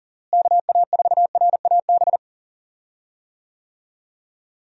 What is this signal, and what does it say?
Morse code, heard as a keyed tone, about 37 wpm, 700 Hz. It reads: KA4RAB
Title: USA Call Signs - 6 Characters - Repeated 2x - 40wpm